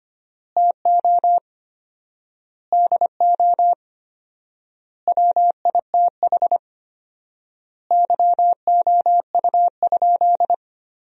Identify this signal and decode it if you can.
Morse code, heard as a keyed tone, about 25 wpm, 700 Hz. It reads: TO DO WITH YOU?
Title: 4x Word-Spacing - Sentences from Top 100 Words - 25wpm